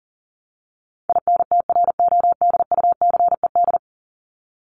Morse code, heard as a keyed tone, about 37 wpm, 700 Hz. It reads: INTRODUCED
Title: ED Suffix - Top 100 Words with Base Words 40wpm